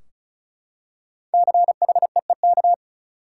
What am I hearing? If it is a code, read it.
Morse code, heard as a keyed tone, about 34 wpm, 700 Hz. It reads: CHEEK
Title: Five Letter Words - 35wpm